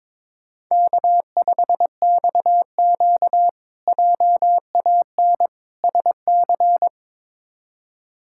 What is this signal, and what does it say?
Morse code, heard as a keyed tone, about 22 wpm, 700 Hz. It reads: K5XQ JAN SC
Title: K1USN SST QSO Practice - 22wpm